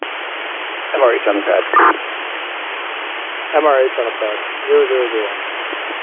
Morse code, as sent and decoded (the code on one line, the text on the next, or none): none